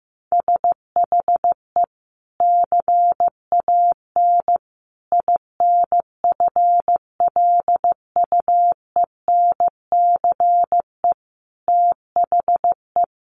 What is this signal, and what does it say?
Morse code, heard as a keyed tone, about 15 wpm, 700 Hz. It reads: SHE CAN INFLUENCE THE
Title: Sentences from Top 1100 Words 15wpm